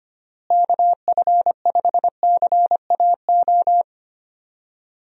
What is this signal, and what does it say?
Morse code, heard as a keyed tone, about 25 wpm, 700 Hz. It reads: KF5CAO
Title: USA Call Signs - 6 Characters - Repeated 3x - 25wpm